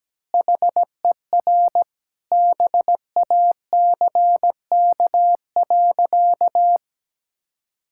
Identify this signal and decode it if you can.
Morse code, heard as a keyed tone, about 17 wpm, 700 Hz. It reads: HER BACK.
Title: Sentences from Top 1000 Words - Review Part 6 - 17wpm